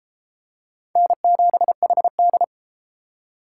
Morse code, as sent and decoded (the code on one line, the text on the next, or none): -. --... .... -..
N7HD